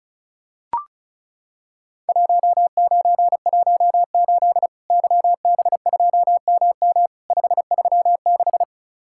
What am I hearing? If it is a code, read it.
Morse code, heard as a keyed tone, about 34 wpm, 700 Hz. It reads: 1918 YB2MM 536
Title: Worked All Europe DX Contest - QTC Exchange - 35wpm